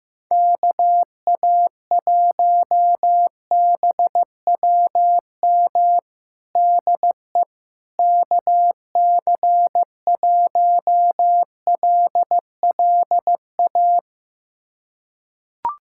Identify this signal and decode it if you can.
Morse code, heard as a keyed tone, about 16 wpm, 700 Hz. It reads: KA1BWM DE KC1LLA
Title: USA Callsign DE Callsign 15wpm